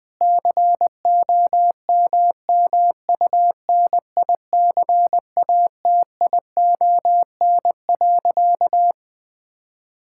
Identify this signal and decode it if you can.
Morse code, heard as a keyed tone, about 21 wpm, 700 Hz. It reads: COMMUNICATION.
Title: Difficult Sentences with Hints - 20wpm